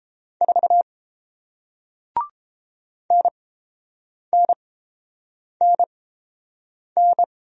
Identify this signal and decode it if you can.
Morse code, heard as a keyed tone, about 29 wpm, spaced as for 7 wpm, 700 Hz. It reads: 4 NNNN